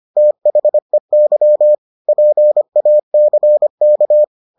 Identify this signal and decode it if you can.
Morse code, heard as a keyed tone, about 25 wpm, 600 Hz. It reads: THEY PACK